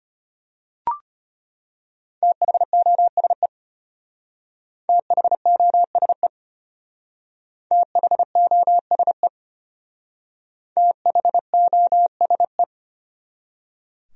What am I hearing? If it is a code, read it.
Morse code, heard as a keyed tone, about 31 wpm, 700 Hz. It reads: THOSE THOSE THOSE THOSE
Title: Speed-Racing Top 500 Words 25wpm